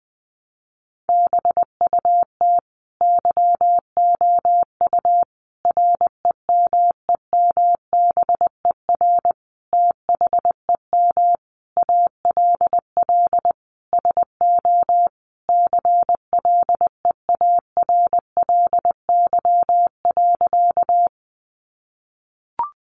Morse code, heard as a keyed tone, about 20 wpm, 700 Hz. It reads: BUT YOU REMEMBER THEM ALL SO CLEARLY.
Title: Sentences from Top 1300 Words 20wpm